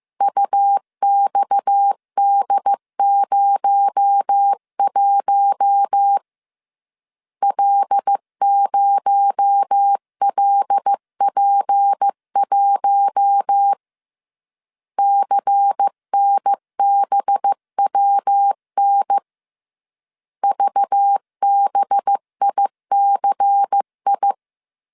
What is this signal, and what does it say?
Morse code, heard as a keyed tone, about 15 wpm, 800 Hz. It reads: UXD01 L0LP1 CNBWN VBICI
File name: CW / DET / 1 6 1500C S C 14wpm 2951